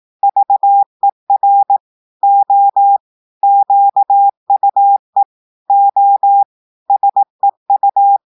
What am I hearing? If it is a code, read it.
Morse code, heard as a keyed tone, about 18 wpm, 800 Hz. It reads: VER O QUE O SEU
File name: portugais texte en clair 2 vitesse 18 mots minutes